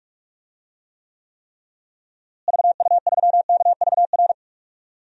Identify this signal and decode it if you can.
Morse code, heard as a keyed tone, about 46 wpm, 700 Hz. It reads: VU3KVR
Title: DXCC-100 Call Signs - 6 Characters - Repeated 3x - 45wpm